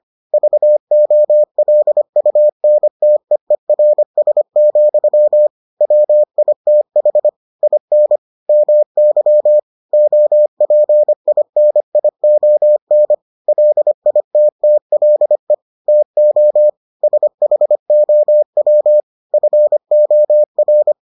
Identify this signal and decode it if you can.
Morse code, heard as a keyed tone, about 25 wpm, 600 Hz. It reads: VOLUNTEERS, WITH IN MY OPINION LITTLE TO SHOW FOR